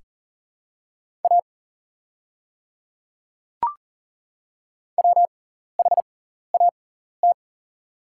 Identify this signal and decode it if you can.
Morse code, heard as a keyed tone, about 38 wpm, spaced as for 14 wpm, 690 Hz. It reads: A WHAT
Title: 6x Character Spacing - Top 100 Words - 40wpm